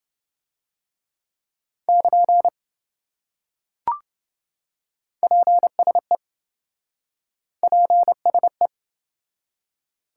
Morse code, heard as a keyed tone, about 27 wpm, 700 Hz. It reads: <KN> PSE PSE